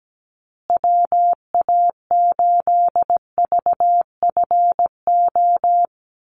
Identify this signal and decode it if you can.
Morse code, heard as a keyed tone, about 17 wpm, 700 Hz. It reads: WA8VFO
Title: USA Call Signs Repeated 3x 17wpm